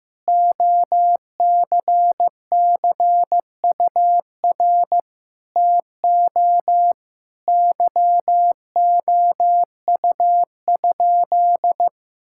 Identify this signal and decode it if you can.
Morse code, heard as a keyed tone, about 15 wpm, 700 Hz. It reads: OCCUR TO YOU?
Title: Sentences from Top 1000 Words - Review Part 6 - 15wpm